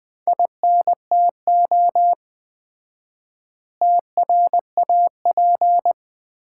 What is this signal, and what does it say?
Morse code, heard as a keyed tone, about 20 wpm, 700 Hz. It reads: INTO TRAP
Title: Common QSO Phrases - 4x Word-Spacing - 20wpm